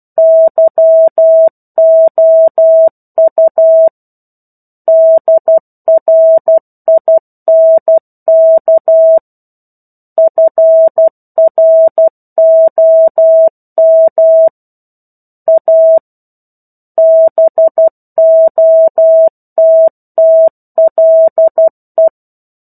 Morse code, heard as a keyed tone, about 12 wpm, 650 Hz. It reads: YOU DRINK FROM A BOTTLE
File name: 01-Chapter-01 12 WPM